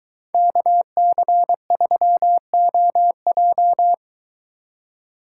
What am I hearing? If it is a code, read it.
Morse code, heard as a keyed tone, about 23 wpm, 700 Hz. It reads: KC3OJ